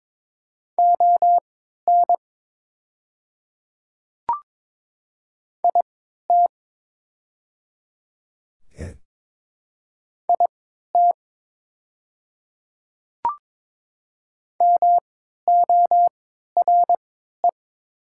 Morse code, heard as a keyed tone, about 22 wpm, spaced as for 12 wpm, 700 Hz. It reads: ON IT IT MORE
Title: 3x Character Spacing - Top 100 Words - 22wpm